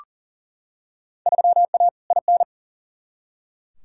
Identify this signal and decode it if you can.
Morse code, heard as a keyed tone, about 40 wpm, 700 Hz. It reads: 3A IN
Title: Field Day Exchanges - 40wpm